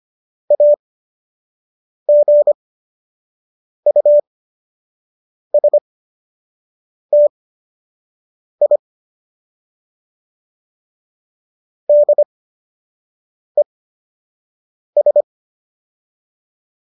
Morse code, heard as a keyed tone, about 25 wpm, spaced as for 6 wpm, 590 Hz. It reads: AGUSTI DES